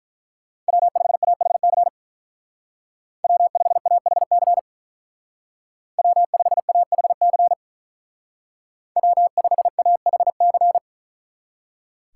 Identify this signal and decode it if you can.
Morse code, heard as a keyed tone, about 44 wpm, 700 Hz. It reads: W5AHC W5AHC W5AHC W5AHC
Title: Speed Racing - USA 1x3 Call Signs 35wpm